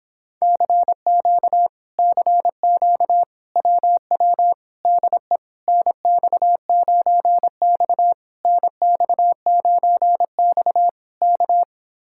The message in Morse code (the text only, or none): CQ CQ WW DE NX9X NX9X K